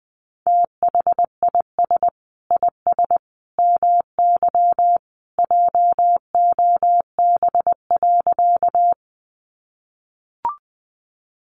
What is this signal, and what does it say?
Morse code, heard as a keyed tone, about 20 wpm, 700 Hz. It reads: THIS IS MY JOB.
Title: Sentences from Top 600 Words 20wpm